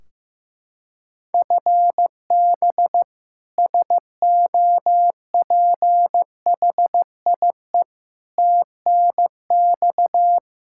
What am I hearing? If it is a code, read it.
Morse code, heard as a keyed tone, about 16 wpm, 700 Hz. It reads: FB SOPHIE TNX
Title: K1USN SST QSO Practice 15wpm